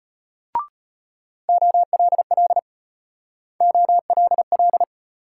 Morse code, heard as a keyed tone, about 32 wpm, 700 Hz. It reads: OLL OLL